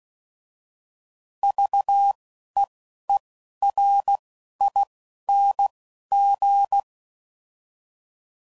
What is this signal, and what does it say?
Morse code, heard as a keyed tone, about 16 wpm, 800 Hz. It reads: V E E R I N G
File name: Prefixes / By First Letter / Vs